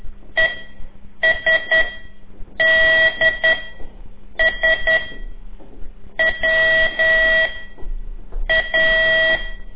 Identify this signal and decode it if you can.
Morse code, heard as a keyed tone, about 10 wpm, 1950 Hz. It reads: E S D S W A
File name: CORSI CW